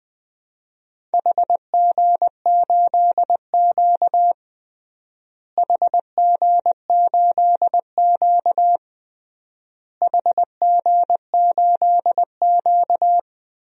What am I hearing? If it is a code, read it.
Morse code, heard as a keyed tone, about 20 wpm, 700 Hz. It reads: HG8Q HG8Q HG8Q
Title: DXCC-100 Call Signs - 4 Characters - Repeated 3x - 20wpm